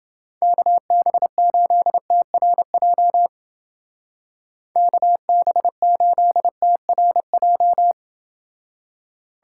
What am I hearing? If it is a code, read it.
Morse code, heard as a keyed tone, about 28 wpm, 700 Hz. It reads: KB8TRJ KB8TRJ